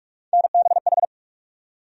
Morse code, heard as a keyed tone, about 46 wpm, 700 Hz. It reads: NBH
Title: USA Call Sign Postfixes - 3 Letters 45wpm